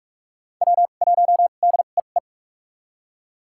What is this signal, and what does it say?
Morse code, heard as a keyed tone, about 44 wpm, spaced as for 32 wpm, 700 Hz. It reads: W1DEE
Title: USA Call Signs - 5 Characters - 2x Character Spacing - 45wpm